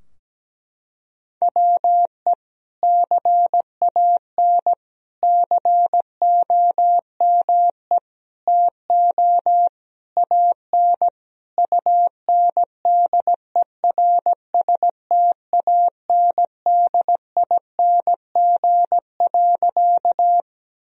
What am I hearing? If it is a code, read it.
Morse code, heard as a keyed tone, about 17 wpm, 700 Hz. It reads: WE CAN COME TO AN UNDERSTANDING.
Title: Sentences from Top 1000 Words - Review Part 3 - 17wpm